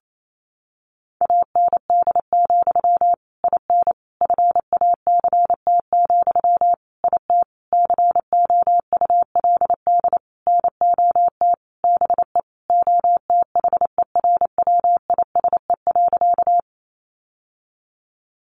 Morse code, heard as a keyed tone, about 28 wpm, 700 Hz. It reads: AND, IN FACT, IT COULD NOT BE OTHERWISE.
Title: Sentences from Top 1200 Words 28wpm